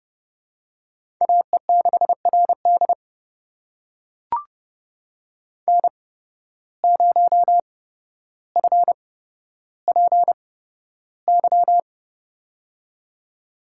Morse code, heard as a keyed tone, about 30 wpm, spaced as for 13 wpm, 700 Hz. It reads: AE6RD N0FPY
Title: USA Call Signs - 5 Characters - 8x Character Spacing - 30wpm